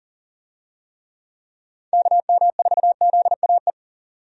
Morse code, heard as a keyed tone, about 40 wpm, 700 Hz. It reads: KM4ZAE
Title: USA Call Signs - 6 Characters - 40wpm